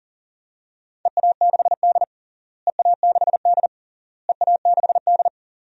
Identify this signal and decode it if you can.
Morse code, heard as a keyed tone, about 40 wpm, 700 Hz. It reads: EA6D EA6D EA6D
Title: DXCC-100 Call Signs - 4 Characters - Repeated 3x - 40wpm